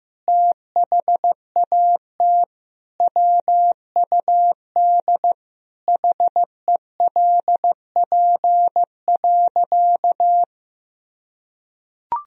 Morse code, heard as a keyed tone, about 15 wpm, 700 Hz. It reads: THAT WUD HELP.
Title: Sentences from Top 200 Words and Abbreviations 15wpm